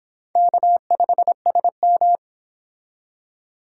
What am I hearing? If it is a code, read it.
Morse code, heard as a keyed tone, about 26 wpm, 700 Hz. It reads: K5SM